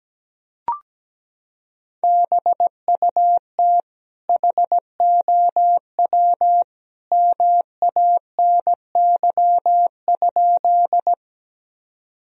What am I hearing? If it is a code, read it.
Morse code, heard as a keyed tone, about 17 wpm, 700 Hz. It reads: BUT HOW MANY?
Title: Sentences from Top 1000 Words - Review Part 1 - 17wpm